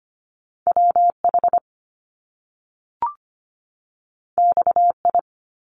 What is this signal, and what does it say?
Morse code, heard as a keyed tone, about 25 wpm, 700 Hz. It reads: WH XI
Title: Top Two Letter Combinations in Words 25wpm